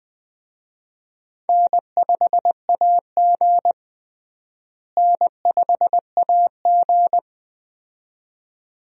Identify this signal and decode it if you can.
Morse code, heard as a keyed tone, about 20 wpm, 700 Hz. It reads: N5AG N5AG
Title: USA Call Signs - 4 Characters - Repeated 2x - 20wpm